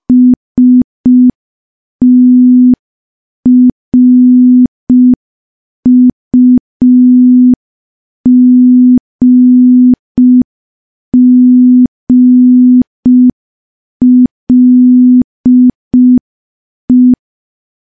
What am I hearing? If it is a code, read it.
Morse code, heard as a keyed tone, about 5 wpm, 250 Hz. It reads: STRUGGLE
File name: morse / morse-training-05WPM